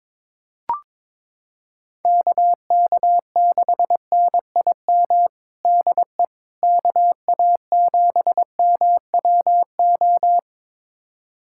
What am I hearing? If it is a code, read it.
Morse code, heard as a keyed tone, about 22 wpm, 700 Hz. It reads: KK6NIM DE KA7MWO
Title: USA Callsign DE Callsign - 6 Characters - 22wpm